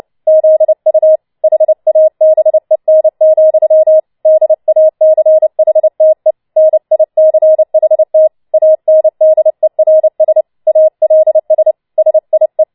Morse code, heard as a keyed tone, about 29 wpm, 630 Hz. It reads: ZU HABEN, DACHTE NICHT ANDERS ALS SIE
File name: Schneewittchen Teil3 30wpm